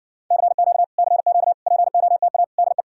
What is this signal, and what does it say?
Morse code, heard as a keyed tone, about 60 wpm, 700 Hz. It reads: CQ CQ POTA DE